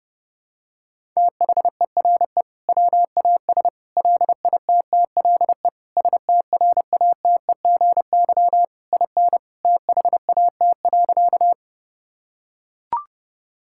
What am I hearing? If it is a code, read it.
Morse code, heard as a keyed tone, about 32 wpm, 700 Hz. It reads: THERE WAS LITTLE STRATEGY IN THAT.
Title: Sentences from Top 1000 Words - Review Part 5 - 30wpm